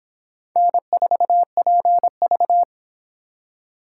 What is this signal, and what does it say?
Morse code, heard as a keyed tone, about 25 wpm, 700 Hz. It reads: N4PV